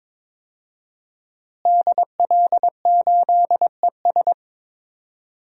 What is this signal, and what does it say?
Morse code, heard as a keyed tone, about 22 wpm, 700 Hz. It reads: DL8ES
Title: DXCC-100 Call Signs - 5 Characters - Repeated 3x - 22wpm